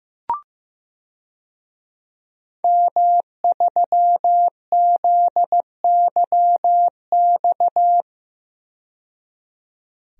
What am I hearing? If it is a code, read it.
Morse code, heard as a keyed tone, about 15 wpm, 700 Hz. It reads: M3ZYX
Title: DXCC-100 Call Signs - 5 Characters - 15wpm